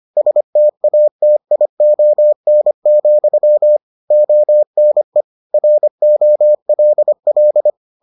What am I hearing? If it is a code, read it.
Morse code, heard as a keyed tone, about 25 wpm, 600 Hz. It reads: STATION, ONE ROLL